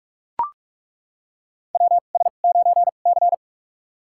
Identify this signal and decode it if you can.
Morse code, heard as a keyed tone, about 43 wpm, spaced as for 32 wpm, 700 Hz. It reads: WS9C
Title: USA Call Signs - 4 Characters - 2x Character Spacing - 45wpm